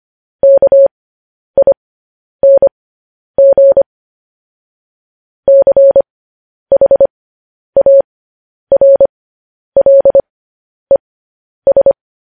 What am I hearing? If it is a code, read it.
Morse code, heard as a keyed tone, about 25 wpm, spaced as for 10 wpm, 550 Hz. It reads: KING CHARLES
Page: news-headlines-in-morse-code-at-10-wpm